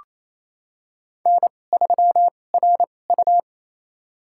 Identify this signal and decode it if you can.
Morse code, heard as a keyed tone, about 28 wpm, spaced as for 20 wpm, 700 Hz. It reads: N3RU
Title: USA Call Signs - 4 Characters - 2x Character Spacing - 28wpm